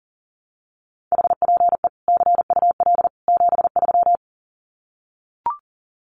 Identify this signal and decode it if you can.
Morse code, heard as a keyed tone, about 40 wpm, 700 Hz. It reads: HPE CUL 73
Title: Common QSO Phrases 40wpm